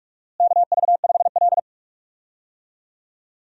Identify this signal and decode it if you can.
Morse code, heard as a keyed tone, about 46 wpm, 700 Hz. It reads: KV5L
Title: USA Call Signs - 4 Characters - Repeated 3x - 45wpm